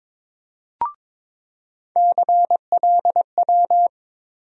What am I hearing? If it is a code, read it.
Morse code, heard as a keyed tone, about 22 wpm, 700 Hz. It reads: CLW
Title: USA Call Sign Postfixes - 3 Letters 22wpm